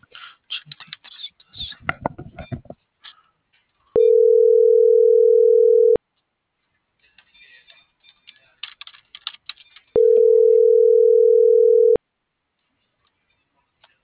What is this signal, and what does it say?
Ringback tone (North America), 440 + 480 Hz.